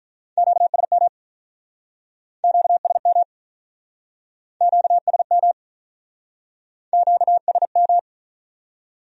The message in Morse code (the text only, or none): QSM QSM QSM QSM